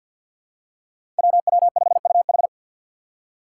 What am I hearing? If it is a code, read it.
Morse code, heard as a keyed tone, about 52 wpm, 700 Hz. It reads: WW5UH
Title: Five Challenging Letter-Number Combinations 50wpm